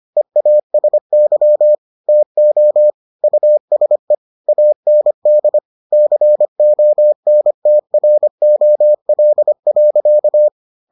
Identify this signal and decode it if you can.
Morse code, heard as a keyed tone, about 25 wpm, 600 Hz. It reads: EASY TO USE AND CONTROL.